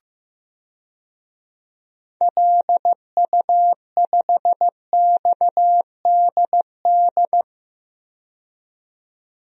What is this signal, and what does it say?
Morse code, heard as a keyed tone, about 15 wpm, 700 Hz. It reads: LU5XDD
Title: DXCC-100 Call Signs - 6 Characters - 15wpm